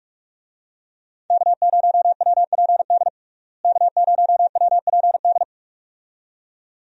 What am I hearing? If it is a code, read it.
Morse code, heard as a keyed tone, about 46 wpm, 700 Hz. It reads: K0WPD K0WPD
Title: USA Call Signs - 5 Characters - Repeated 2x - 45wpm